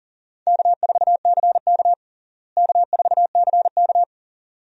Morse code, heard as a keyed tone, about 37 wpm, 700 Hz. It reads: K4CK K4CK
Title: USA Call Signs - 4 Characters - Repeated 2x - 40wpm